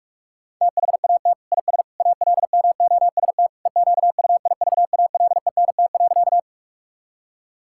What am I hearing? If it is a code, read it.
Morse code, heard as a keyed tone, about 41 wpm, 700 Hz. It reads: THAT IS ALMOST EQUIVALENT.
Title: Sentences from Top 1600 Words 45wpm